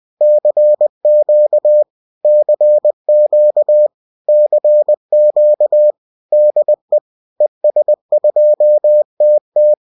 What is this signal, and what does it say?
Morse code, heard as a keyed tone, about 20 wpm, 600 Hz. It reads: CQ CQ CQ DE ES2TT